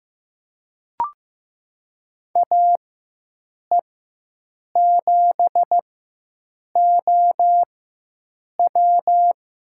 Morse code, heard as a keyed tone, about 15 wpm, spaced as for 7 wpm, 700 Hz. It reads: AE7OW
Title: USA Call Signs - 4x Character Spacing - 15wpm